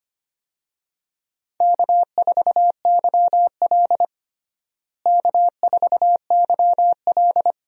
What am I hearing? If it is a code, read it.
Morse code, heard as a keyed tone, about 25 wpm, 700 Hz. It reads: K4YL K4YL
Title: USA Call Signs - 4 Characters - Repeated 2x - 25wpm